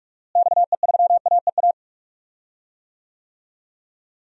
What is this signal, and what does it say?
Morse code, heard as a keyed tone, about 42 wpm, 700 Hz. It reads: XE3AEA
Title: DXCC-100 Call Signs - 6 Characters - Repeated 3x - 45wpm